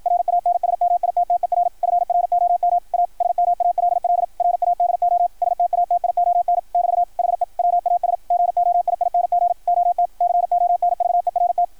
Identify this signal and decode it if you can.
Morse code, heard as a keyed tone, about 53 wpm, 690 Hz. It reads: RANSMITTER FROM A SMALL RADO STATION <BT> HE WAS GOIING OT CONVERT
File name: archived / CRSnet